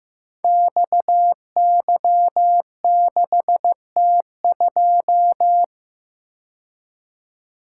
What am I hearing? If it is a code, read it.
Morse code, heard as a keyed tone, about 15 wpm, 700 Hz. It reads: XY6T2
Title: Code Groups - 5 Letter:Numbers - 15wpm